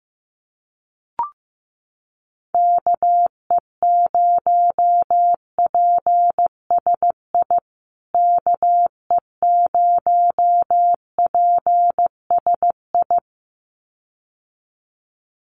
Morse code, heard as a keyed tone, about 15 wpm, 700 Hz. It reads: KE0PSI KE0PSI
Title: USA Call Signs Repeated 2x 15wpm